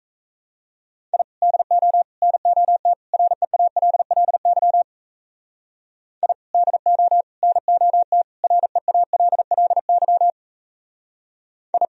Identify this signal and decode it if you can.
Morse code, heard as a keyed tone, about 40 wpm, 700 Hz. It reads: I DO NOT REALLY I DO NOT REALLY I